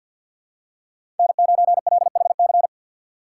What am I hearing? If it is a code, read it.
Morse code, heard as a keyed tone, about 53 wpm, 700 Hz. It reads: N9LHC
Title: USA Call Signs - 6x Character Spacing - 50wpm